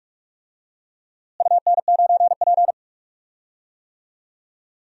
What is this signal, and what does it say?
Morse code, heard as a keyed tone, about 46 wpm, 700 Hz. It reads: UN9P
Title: DXCC-100 Call Signs - 4 Characters - Repeated 2x - 45wpm